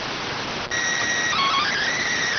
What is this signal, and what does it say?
SSTV (slow-scan television), PD 290.